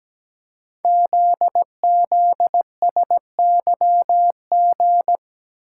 Morse code, heard as a keyed tone, about 17 wpm, 700 Hz. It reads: ZZSYG
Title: Code Groups - 5 Letters - 17wpm